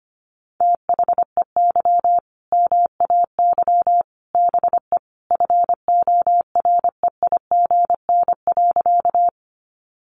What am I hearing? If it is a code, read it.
Morse code, heard as a keyed tone, about 25 wpm, 700 Hz. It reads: THEY MAY BE FOREIGN.
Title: Sentences from Top 1400 Words 25wpm